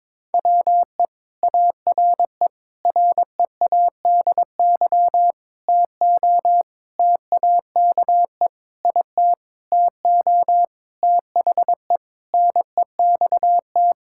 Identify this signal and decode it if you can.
Morse code, heard as a keyed tone, about 22 wpm, 700 Hz. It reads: WE ARE READY TO TAKE IT TO THE NEXT
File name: Sentences with 7-Word Phrases - 22wpm